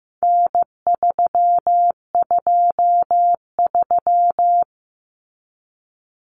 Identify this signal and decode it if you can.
Morse code, heard as a keyed tone, about 15 wpm, 700 Hz. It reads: N323
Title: Common Cut Numbers - 4-Digit 15wpm